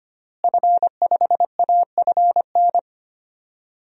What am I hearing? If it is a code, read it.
Morse code, heard as a keyed tone, about 25 wpm, 700 Hz. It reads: F5AFN
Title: DXCC-100 Call Signs - 5 Characters - Repeated 3x - 25wpm